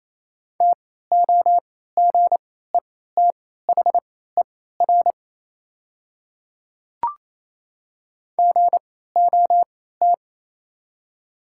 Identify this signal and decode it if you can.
Morse code, heard as a keyed tone, about 28 wpm, spaced as for 16 wpm, 700 Hz. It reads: TOGETHER GOT
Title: 3x Character Spacing - Top 300 Words - 28wpm